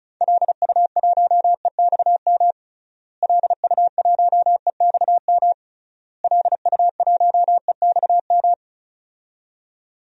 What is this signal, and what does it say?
Morse code, heard as a keyed tone, about 36 wpm, 700 Hz. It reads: LU1EXM LU1EXM LU1EXM
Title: DXCC-100 Call Signs - 6 Characters - Repeated 3x - 35wpm